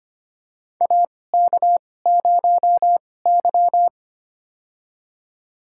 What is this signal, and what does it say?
Morse code, heard as a keyed tone, about 25 wpm, spaced as for 18 wpm, 700 Hz. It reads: AK0Y
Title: USA Call Signs - 2x Character Spacing - 25wpm